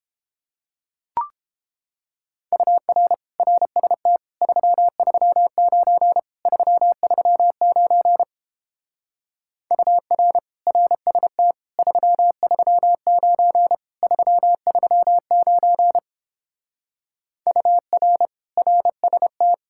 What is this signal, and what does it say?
Morse code, heard as a keyed tone, about 31 wpm, 700 Hz. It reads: UR RST 339 339 UR RST 339 339 UR RST